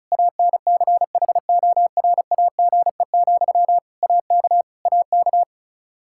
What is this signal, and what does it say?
Morse code, heard as a keyed tone, about 36 wpm, 700 Hz. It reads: ANCHORAGE, AK AK
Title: Practice QSO - Exchange 1 - Beginning - 35wpm